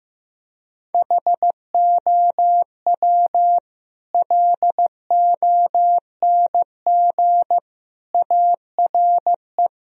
Morse code, heard as a keyed tone, about 16 wpm, 700 Hz. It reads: HOW LONG ARE
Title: Sentences from Top 1000 Words - Review Part 2 - 15wpm